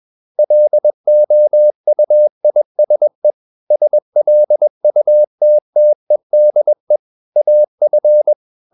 Morse code, heard as a keyed tone, about 21 wpm, 600 Hz. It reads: LOUISE SLUTTEDE AF
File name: Ferie QRS 21wpm